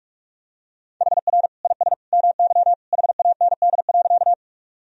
Morse code, heard as a keyed tone, about 46 wpm, 700 Hz. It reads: HR IS MY HAND.
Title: Sentences from Top 200 Words and Abbreviations 45wpm